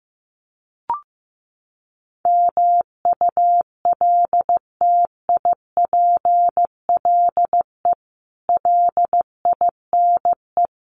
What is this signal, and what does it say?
Morse code, heard as a keyed tone, about 15 wpm, 700 Hz. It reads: MULTIPLE LINE
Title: Sets of 5 Words 15wpm